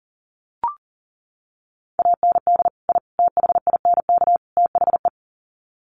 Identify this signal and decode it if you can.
Morse code, heard as a keyed tone, about 37 wpm, 700 Hz. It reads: AND I THINK THE
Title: Sets of 4 Words 40wpm